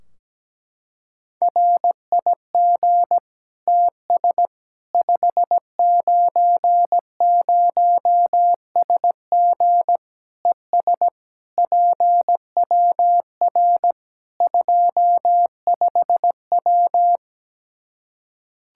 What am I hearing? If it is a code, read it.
Morse code, heard as a keyed tone, about 17 wpm, 700 Hz. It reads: RIG TS 590SG ES PWR 25W
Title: Practice QSO - Exchange 2 - Info - 17wpm